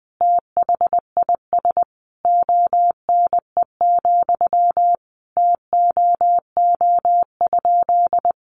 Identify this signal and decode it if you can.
Morse code, heard as a keyed tone, about 20 wpm, 700 Hz. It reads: THIS ONE, TOO?
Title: Sentences from Top 200 Words 20wpm